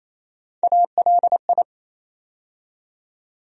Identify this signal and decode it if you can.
Morse code, heard as a keyed tone, about 28 wpm, 700 Hz. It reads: ALI
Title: USA Call Sign Postfixes - 3 Letters 28wpm